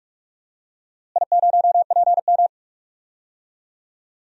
Morse code, heard as a keyed tone, about 46 wpm, 700 Hz. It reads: I0PM
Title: DXCC-100 Call Signs - 4 Characters - Repeated 3x - 45wpm